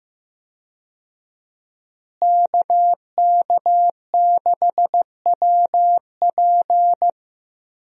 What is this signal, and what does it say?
Morse code, heard as a keyed tone, about 15 wpm, 700 Hz. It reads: KK6WP
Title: USA Call Signs - 5 Characters - Repeated 3x - 15wpm